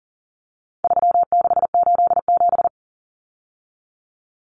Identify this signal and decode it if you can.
Morse code, heard as a keyed tone, about 40 wpm, 700 Hz. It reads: 3687
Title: Numbers - 4-Digit 40wpm